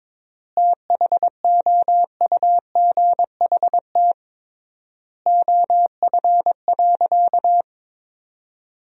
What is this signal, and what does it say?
Morse code, heard as a keyed tone, about 22 wpm, 700 Hz. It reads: THOUGHT OF.
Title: 3x Word-Spacing - Sentences from Top 300 Words - 22wpm